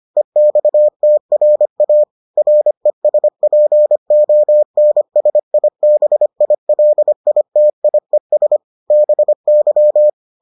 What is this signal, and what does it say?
Morse code, heard as a keyed tone, about 25 wpm, 600 Hz. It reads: EXTRA RESPONSIBILITIES BY